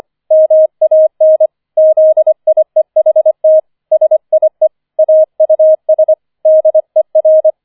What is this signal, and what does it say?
Morse code, heard as a keyed tone, about 25 wpm, 630 Hz. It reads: MAN ZIEHT SIE AUS DER
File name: 25wpm